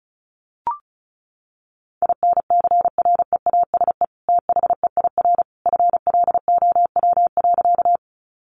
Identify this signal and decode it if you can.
Morse code, heard as a keyed tone, about 34 wpm, 700 Hz. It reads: INCREASE THEIR FLOW.
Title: Sentences from Top 1200 Words 35wpm